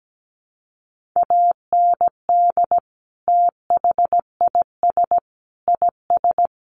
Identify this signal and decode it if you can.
Morse code, heard as a keyed tone, about 17 wpm, 700 Hz. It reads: AND THIS IS
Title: Sentences from Top 1400 Words 17wpm